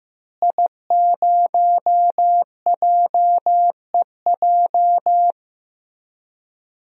Morse code, heard as a keyed tone, about 15 wpm, 700 Hz. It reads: I0JEJ
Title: DXCC-100 Call Signs - 5 Characters - Repeated 2x - 15wpm